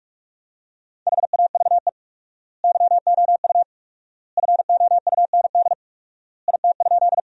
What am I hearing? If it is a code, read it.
Morse code, heard as a keyed tone, about 46 wpm, 700 Hz. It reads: HAVE YOU FOUND IT?
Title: 4x Word-Spacing - Sentences from Top 300 Words - 45wpm